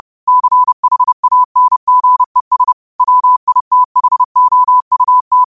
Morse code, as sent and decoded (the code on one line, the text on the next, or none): -.-. .... .- -. --. ... / .-- .. - .... --- ..- -
CHANGES WITHOUT